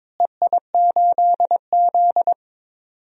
Morse code, heard as a keyed tone, about 22 wpm, 700 Hz. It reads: EI8Z
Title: DXCC-100 Call Signs - 4 Characters - Repeated 3x - 22wpm